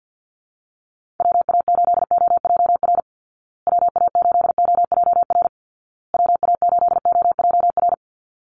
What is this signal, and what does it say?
Morse code, heard as a keyed tone, about 50 wpm, 700 Hz. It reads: WA8OJR WA8OJR WA8OJR
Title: Call Signs Repeated 2x - 50wpm